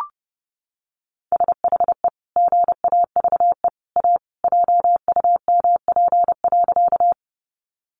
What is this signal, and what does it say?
Morse code, heard as a keyed tone, about 30 wpm, 700 Hz. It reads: SHE GAVE A JUMP.
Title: Sentences from Top 1300 Words 30wpm